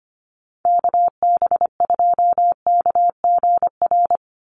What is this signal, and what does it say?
Morse code, heard as a keyed tone, about 25 wpm, 700 Hz. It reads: KB2KGR
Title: USA Call Signs 25wpm